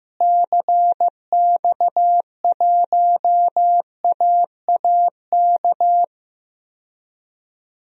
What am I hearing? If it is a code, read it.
Morse code, heard as a keyed tone, about 15 wpm, 700 Hz. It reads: CX1AAK
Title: DXCC-100 Call Signs - 6 Characters - Repeated 2x - 15wpm